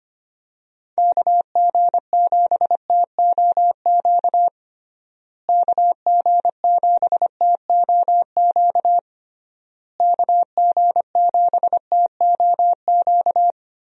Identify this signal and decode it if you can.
Morse code, heard as a keyed tone, about 25 wpm, 700 Hz. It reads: KG7TOQ KG7TOQ KG7TOQ
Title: USA Call Signs - 6 Characters - Repeated 3x - 25wpm